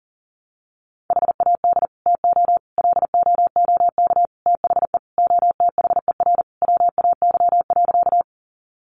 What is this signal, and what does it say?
Morse code, heard as a keyed tone, about 40 wpm, 700 Hz. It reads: HAD TO LOOK THE OTHER WAY.
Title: Sentences from Top 100 Words 40wpm